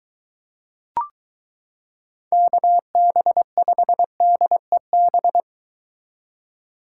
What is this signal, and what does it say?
Morse code, heard as a keyed tone, about 23 wpm, 700 Hz. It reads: KB5DEB